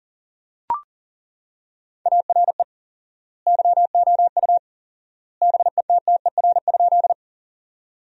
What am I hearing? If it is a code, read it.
Morse code, heard as a keyed tone, about 40 wpm, 700 Hz. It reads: ARE YOU BETTER?
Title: 4x Word-Spacing - Sentences from Top 400 Words - 40wpm